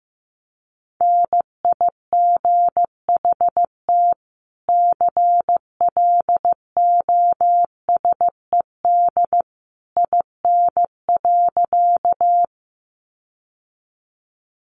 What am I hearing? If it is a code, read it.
Morse code, heard as a keyed tone, about 15 wpm, 700 Hz. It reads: NIGHT CLOSED IN.
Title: Sentences from Top 1400 Words 15wpm